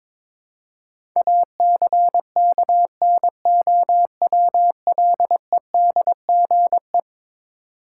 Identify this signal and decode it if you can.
Morse code, heard as a keyed tone, about 22 wpm, 700 Hz. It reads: ACKNOWLEDGE
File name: practice / 22wpm / Eleven Letter Words - 22wpm